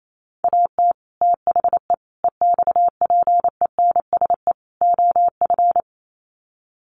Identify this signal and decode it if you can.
Morse code, heard as a keyed tone, about 28 wpm, 700 Hz. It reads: AT THE EXPENSE OF
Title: Sets of 4 Words 28wpm